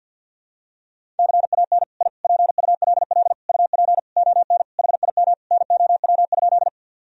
Morse code, heard as a keyed tone, about 50 wpm, 700 Hz. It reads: CAN I PULL UP ON HIM NOW?
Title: Sentences from Top 1000 Words - Review Part 2 - 50wpm